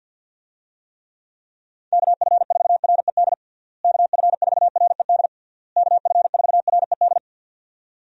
Morse code, heard as a keyed tone, about 52 wpm, 700 Hz. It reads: KF4RED KF4RED KF4RED
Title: USA Call Signs - 6 Characters - Repeated 3x - 50wpm